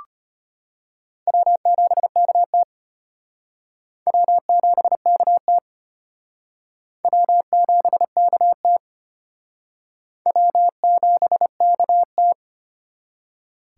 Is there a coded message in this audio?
Yes: Morse code from 1.3 to 12.3 s, heard as a keyed tone, about 32 wpm, 700 Hz. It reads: W7KT W7KT W7KT W7KT